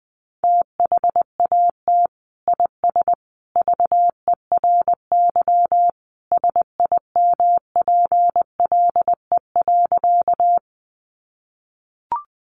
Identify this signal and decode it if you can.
Morse code, heard as a keyed tone, about 20 wpm, 700 Hz. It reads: THAT IS VERY SIMPLE.
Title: Sentences from Top 400 Words 20wpm